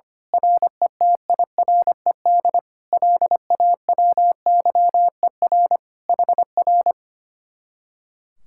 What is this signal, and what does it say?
Morse code, heard as a keyed tone, about 25 wpm, 700 Hz. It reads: RETIRED LAWYER HR
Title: Speed-Racing Common QSO Phrases - 25wpm